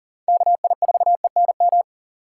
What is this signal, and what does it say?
Morse code, heard as a keyed tone, about 37 wpm, 700 Hz. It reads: KI4ENM
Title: USA Call Signs - 6 Characters - 4x Character Spacing - 40wpm